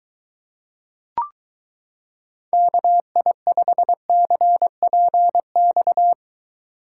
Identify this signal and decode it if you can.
Morse code, heard as a keyed tone, about 22 wpm, 700 Hz. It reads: KI5CPX